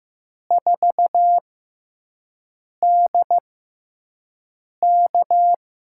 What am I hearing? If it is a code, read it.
Morse code, heard as a keyed tone, about 15 wpm, spaced as for 5 wpm, 700 Hz. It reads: 4DK
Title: USA Call Signs - 5 Characters - 6x Character Spacing - 15wpm